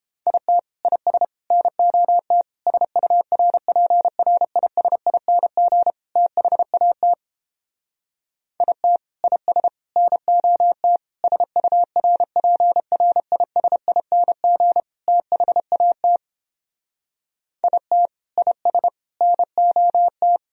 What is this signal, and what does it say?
Morse code, heard as a keyed tone, about 29 wpm, 700 Hz. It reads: IT IS NOT SURPRISING THAT IT IS NOT SURPRISING THAT IT IS NOT